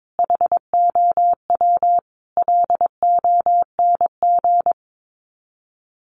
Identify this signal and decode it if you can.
Morse code, heard as a keyed tone, about 22 wpm, 700 Hz. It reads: HOW LONG
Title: Sets of 5 Words Encore2 22wpm